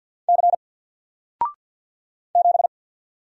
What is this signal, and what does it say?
Morse code, heard as a keyed tone, about 48 wpm, 700 Hz. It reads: C 7
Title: Single Character (Letter:Number:Punctuation:Prosign) Alt 50wpm